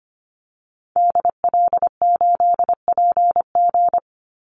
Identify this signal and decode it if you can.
Morse code, heard as a keyed tone, about 25 wpm, 700 Hz. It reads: DL8PG
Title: Call Signs Repeated 2x - 25wpm